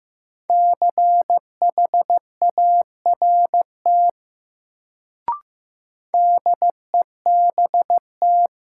Morse code, heard as a keyed tone, about 16 wpm, 700 Hz. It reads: CHART DEBT
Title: Sentences from Top 1300 Words - New Word Drill - 15wpm